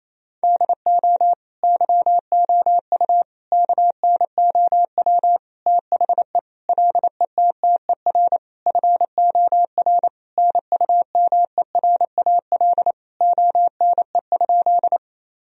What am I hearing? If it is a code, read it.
Morse code, heard as a keyed tone, about 27 wpm, 700 Hz. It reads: DO YOU KNOW THE LETTER FOR NUMERAL ONE?
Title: Sentences from Top 1000 Words - Review Part 2 - 28wpm